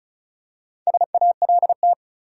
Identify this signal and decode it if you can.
Morse code, heard as a keyed tone, about 34 wpm, 700 Hz. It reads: SALT
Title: Four Letter Words - 35wpm